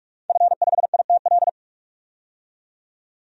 Morse code, heard as a keyed tone, about 47 wpm, 700 Hz. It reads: F5ITL
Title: DXCC-100 Call Signs - 5 Characters - 45wpm